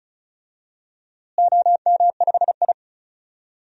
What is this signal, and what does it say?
Morse code, heard as a keyed tone, about 34 wpm, 700 Hz. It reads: OM5I
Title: DXCC-100 Call Signs - 4 Characters - Repeated 3x - 35wpm